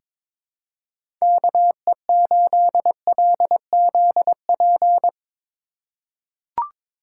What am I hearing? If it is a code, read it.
Morse code, heard as a keyed tone, about 22 wpm, 700 Hz. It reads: KE8LZP
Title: USA Call Signs - 6 Characters - 3x Character Spacing - 22wpm